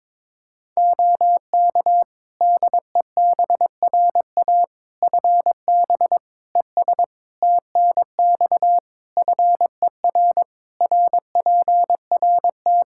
Morse code, heard as a keyed tone, about 22 wpm, 700 Hz. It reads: OK DEBRA FB ES TNX FER RPRT
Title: Practice QSO - Exchange 2 - Info - 22wpm